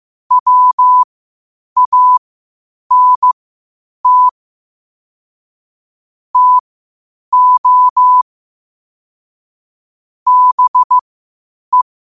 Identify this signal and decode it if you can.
Morse code, heard as a keyed tone, about 15 wpm, spaced as for 8 wpm, 1000 Hz. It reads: WANT TO BE
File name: qotdfiles / 8wpm